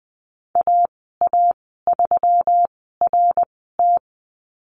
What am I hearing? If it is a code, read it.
Morse code, heard as a keyed tone, about 19 wpm, spaced as for 14 wpm, 700 Hz. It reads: AA3RT
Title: 167 Lesson 27g - Number 7 - Call Signs 20wpm